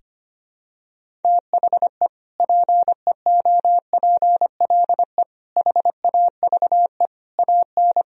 Morse code, heard as a keyed tone, about 25 wpm, 700 Hz. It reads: THE PEOPLE HAVE AN
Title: Sentences from Top 1000 Words - Review Part 1 - 25wpm